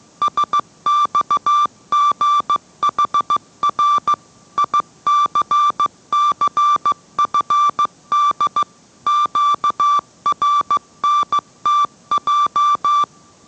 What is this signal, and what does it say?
Morse code, heard as a keyed tone, about 15 wpm, 1170 Hz. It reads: SXGHR ICCFD QRNTJ